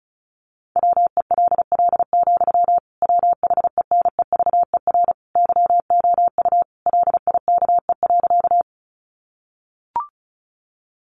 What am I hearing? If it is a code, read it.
Morse code, heard as a keyed tone, about 34 wpm, 700 Hz. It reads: WELL, WHENEVER YOU LIKE.
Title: Sentences from Top 1200 Words 35wpm